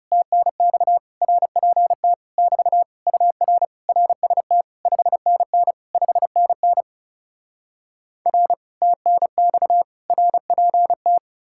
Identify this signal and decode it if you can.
Morse code, heard as a keyed tone, about 40 wpm, 700 Hz. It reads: TNX RPT <BT> UR RST 5NN 5NN R TNX RPT